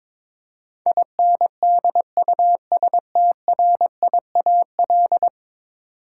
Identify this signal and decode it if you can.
Morse code, heard as a keyed tone, about 22 wpm, 700 Hz. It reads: INDUSTRIAL
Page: morse-code-ninja-practice-22wpm